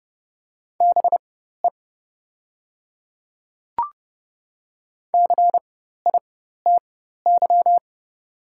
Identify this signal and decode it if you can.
Morse code, heard as a keyed tone, about 30 wpm, spaced as for 14 wpm, 700 Hz. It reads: BE CITY
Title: 4x Character Spacing - Top 300 Words - 30wpm